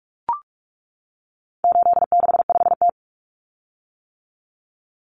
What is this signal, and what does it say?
Morse code, heard as a keyed tone, about 44 wpm, 700 Hz. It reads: 865T
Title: Common Cut Numbers - 4-Digit 45wpm